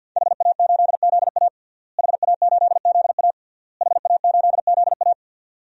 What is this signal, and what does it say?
Morse code, heard as a keyed tone, about 52 wpm, 700 Hz. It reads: HA8ZA HA8ZA HA8ZA
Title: DXCC-100 Call Signs - 5 Characters - Repeated 3x - 50wpm